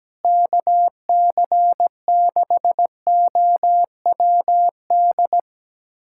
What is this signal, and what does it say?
Morse code, heard as a keyed tone, about 17 wpm, 700 Hz. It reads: KC6OWD
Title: USA Callsign DE Callsign - 6 Characters - 17wpm